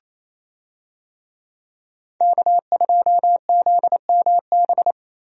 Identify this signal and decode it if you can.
Morse code, heard as a keyed tone, about 28 wpm, 700 Hz. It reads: K2ZMB
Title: USA Call Signs - 5 Characters - Repeated 3x - 28wpm